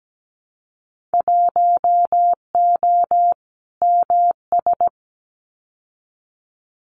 Morse code, heard as a keyed tone, about 17 wpm, 690 Hz. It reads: 1O MS
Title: Winter Field Day Exchanges 17wpm